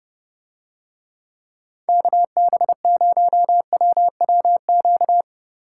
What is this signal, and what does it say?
Morse code, heard as a keyed tone, about 30 wpm, 700 Hz. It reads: KB0WWQ